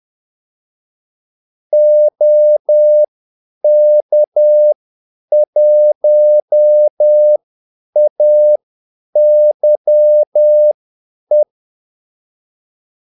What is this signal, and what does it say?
Morse code, heard as a keyed tone, about 10 wpm, spaced as for 8 wpm, 600 Hz. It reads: OK1AYE
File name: Callsigns-10-8wpm